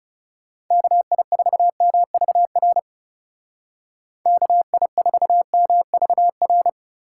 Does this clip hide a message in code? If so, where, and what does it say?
Yes: Morse code from 0.7 to 6.7 s, heard as a keyed tone, about 40 wpm, 700 Hz. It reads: KI4MVR KI4MVR